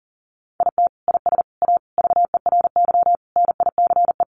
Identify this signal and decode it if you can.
Morse code, heard as a keyed tone, about 40 wpm, 700 Hz. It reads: IT IS A VERY NICE
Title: Sentences from Top 1100 Words 40wpm